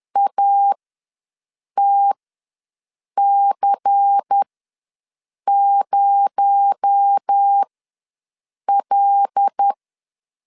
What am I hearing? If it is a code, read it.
Morse code, heard as a keyed tone, about 10 wpm, spaced as for 6 wpm, 800 Hz. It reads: ATC0L
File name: LEZIONI / 7 GF9Z / 7 GF9Z ws 10wpm 3919